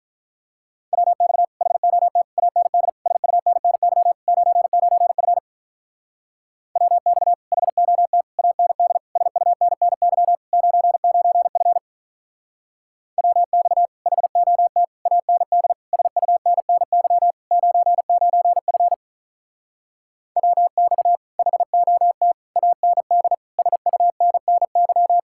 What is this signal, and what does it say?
Morse code, heard as a keyed tone, about 43 wpm, 700 Hz. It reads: WX HOT AND SUNNY 99F WX HOT AND SUNNY 99F WX HOT AND SUNNY 99F WX HOT AND SUNNY
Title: Speed-Racing Common QSO Phrases - 35wpm